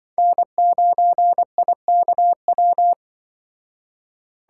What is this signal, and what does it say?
Morse code, heard as a keyed tone, about 24 wpm, 700 Hz. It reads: N9IKW
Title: Speed Racing - USA 1x3 Call Signs 20wpm